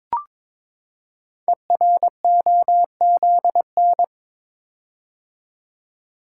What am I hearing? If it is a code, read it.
Morse code, heard as a keyed tone, about 22 wpm, 700 Hz. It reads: EROZN
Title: Code Groups - 5 Letters - 22wpm